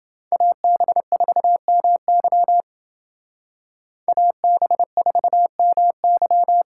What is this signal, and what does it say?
Morse code, heard as a keyed tone, about 28 wpm, 700 Hz. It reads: AB4MY AB4MY